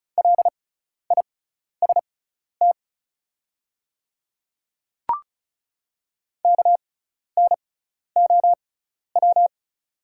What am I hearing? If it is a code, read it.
Morse code, heard as a keyed tone, about 34 wpm, spaced as for 12 wpm, 700 Hz. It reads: LIST KNOW
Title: 6x Character Spacing - Top 500 Words - 35wpm